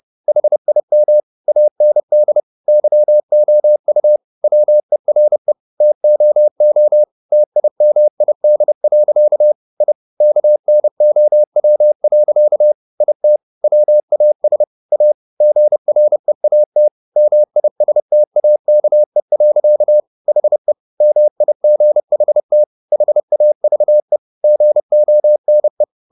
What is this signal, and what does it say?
Morse code, heard as a keyed tone, about 30 wpm, 600 Hz. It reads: HIM AND YOU WERE TOO TIMID. I KNOW. IT WAS A GREAT MISTAKE. HE MIGHT HAVE GONE
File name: the old man and the sea / the old man and the sea 1 30wpm